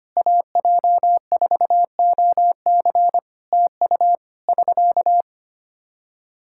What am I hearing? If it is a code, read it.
Morse code, heard as a keyed tone, about 25 wpm, 700 Hz. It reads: AJ4OC TU <SK>
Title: Practice QSO - Exchange 4 - Ending - 25wpm